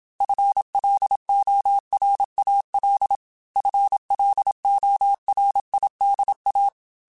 Morse code, heard as a keyed tone, about 27 wpm, 800 Hz. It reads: FLORAL FLORIDA
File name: FL Runs / FL Runs mp3s / FL Runs @25